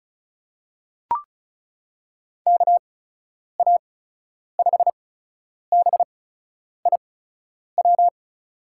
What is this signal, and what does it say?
Morse code, heard as a keyed tone, about 34 wpm, spaced as for 10 wpm, 700 Hz. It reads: KA5BIW
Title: USA Call Signs - 6 Characters - 8x Character Spacing - 35wpm